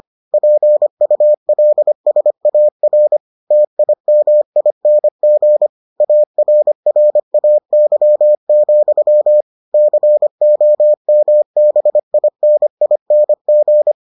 Morse code, heard as a keyed tone, about 25 wpm, 600 Hz. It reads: PULSAR TIMING ARRAY, COMBINING